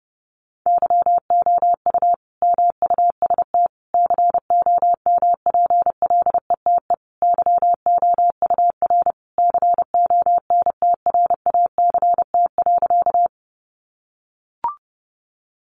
Morse code, heard as a keyed tone, about 28 wpm, 700 Hz. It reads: YOU MUST COMPLETE YOUR CONTRACT.
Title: Sentences from Top 1400 Words 30wpm